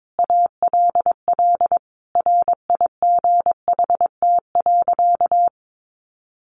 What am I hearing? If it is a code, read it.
Morse code, heard as a keyed tone, about 22 wpm, 700 Hz. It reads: ALL RIGHT.
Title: Sentences from Top 1200 Words 22wpm